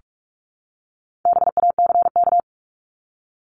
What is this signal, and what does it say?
Morse code, heard as a keyed tone, about 44 wpm, 700 Hz. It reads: BACK
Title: Top 500 Words 45wpm